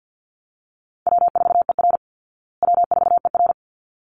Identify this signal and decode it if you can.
Morse code, heard as a keyed tone, about 48 wpm, 700 Hz. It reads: W4ER W4ER
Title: Call Signs Repeated 3x - 50wpm